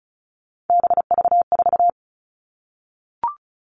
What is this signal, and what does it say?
Morse code, heard as a keyed tone, about 34 wpm, 700 Hz. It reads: BV4
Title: All Cut Numbers - 3-Digit 35wpm